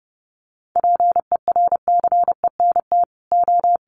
Morse code, heard as a keyed tone, about 30 wpm, 700 Hz. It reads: PERCENT O
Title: Sets of 2 Words 30wpm